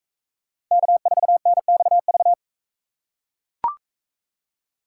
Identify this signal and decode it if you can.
Morse code, heard as a keyed tone, about 42 wpm, 700 Hz. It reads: K4NXV